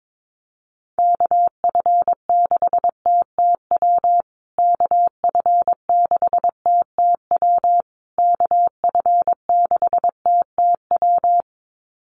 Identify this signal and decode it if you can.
Morse code, heard as a keyed tone, about 22 wpm, 700 Hz. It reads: KF6TTW KF6TTW KF6TTW
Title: USA Call Signs Repeated 3x 22wpm